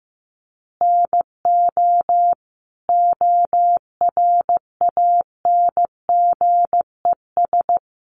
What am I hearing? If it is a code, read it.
Morse code, heard as a keyed tone, about 15 wpm, 700 Hz. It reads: NO ORANGES
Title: Easy Phrases 15wpm